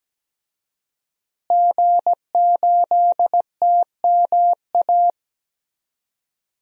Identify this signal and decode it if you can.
Morse code, heard as a keyed tone, about 17 wpm, 700 Hz. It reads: G8TMA
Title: DXCC-100 Call Signs - 5 Characters - Repeated 2x - 17wpm